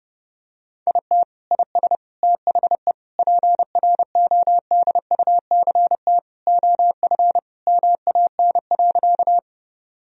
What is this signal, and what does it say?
Morse code, heard as a keyed tone, about 30 wpm, 700 Hz. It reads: IT IS THE PRODUCT OF MAN.
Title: Sentences from Top 1000 Words - Review Part 2 - 30wpm